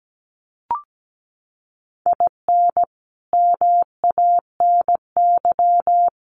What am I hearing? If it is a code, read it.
Morse code, heard as a keyed tone, about 17 wpm, 700 Hz. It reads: IN MANY
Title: Sets of 5 Words Encore 17wpm